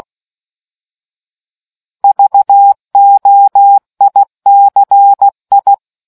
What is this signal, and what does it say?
Morse code, heard as a keyed tone, about 16 wpm, 800 Hz. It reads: VOICI